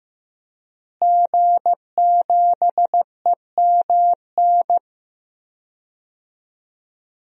Morse code, heard as a keyed tone, about 15 wpm, 700 Hz. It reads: G7EMN
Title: DXCC-100 Call Signs - 5 Characters - Repeated 3x - 15wpm